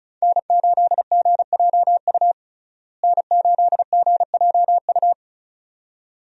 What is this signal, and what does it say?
Morse code, heard as a keyed tone, about 36 wpm, 700 Hz. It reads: N8GJU N8GJU
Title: USA Call Signs - 5 Characters - Repeated 2x - 35wpm